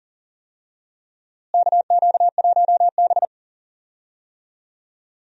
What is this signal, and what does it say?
Morse code, heard as a keyed tone, about 38 wpm, 700 Hz. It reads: KQ1B
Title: USA Call Signs - 4 Characters - Repeated 2x - 40wpm